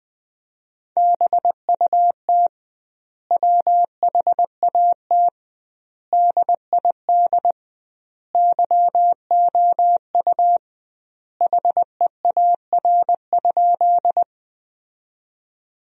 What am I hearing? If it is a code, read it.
Morse code, heard as a keyed tone, about 20 wpm, 700 Hz. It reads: BUT WHAT DID YOU HEAR?
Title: 2x Word-Spacing - Sentences from Top 300 Words - 20wpm